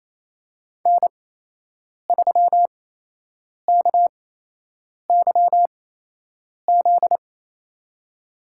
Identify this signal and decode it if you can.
Morse code, heard as a keyed tone, about 28 wpm, spaced as for 8 wpm, 700 Hz. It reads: N3KYZ
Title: USA Call Signs - 8x Character Spacing - 28wpm